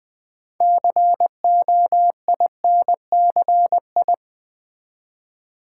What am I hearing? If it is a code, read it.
Morse code, heard as a keyed tone, about 20 wpm, 700 Hz. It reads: COINCI
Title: Build A Word - 3 Letters At A Time - 8-Letter Words - 20wpm